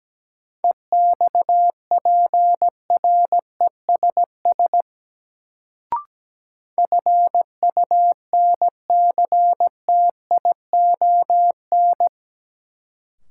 Morse code, heard as a keyed tone, about 17 wpm, 700 Hz. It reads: EXPRESS FUNCTION
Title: Sentences from Top 1100 Words - New Word Drill - 17wpm